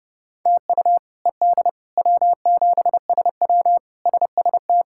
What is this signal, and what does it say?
Morse code, heard as a keyed tone, about 32 wpm, 700 Hz. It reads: TU ED W7SW SST
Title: K1USN SST QSO Practice - 30wpm